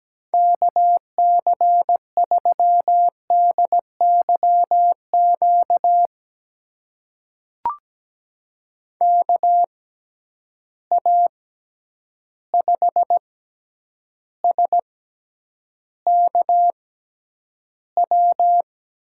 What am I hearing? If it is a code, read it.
Morse code, heard as a keyed tone, about 17 wpm, spaced as for 9 wpm, 700 Hz. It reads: KC3DYQ KA5SKW
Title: USA Call Signs - 6 Characters - 6x Character Spacing - 17wpm